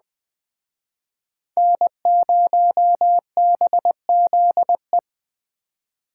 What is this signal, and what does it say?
Morse code, heard as a keyed tone, about 20 wpm, 700 Hz. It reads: N0BZE
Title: Speed Racing - USA 1x3 Call Signs 17wpm